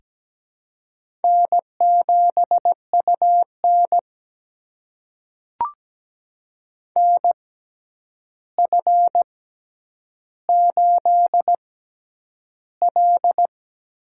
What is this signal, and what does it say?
Morse code, heard as a keyed tone, about 17 wpm, spaced as for 9 wpm, 700 Hz. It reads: N7UN NF8L
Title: USA Call Signs - 6x Character Spacing - 17wpm